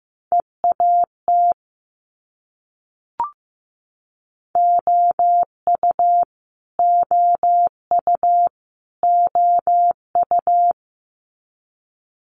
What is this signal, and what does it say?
Morse code, heard as a keyed tone, about 15 wpm, 700 Hz. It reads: EAT OU OU OU
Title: Multi-letter Phonemes Repeated 3x 15wpm